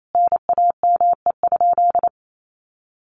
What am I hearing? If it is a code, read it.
Morse code, heard as a keyed tone, about 28 wpm, 700 Hz. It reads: NAME?
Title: Sentences from Top 600 Words 28wpm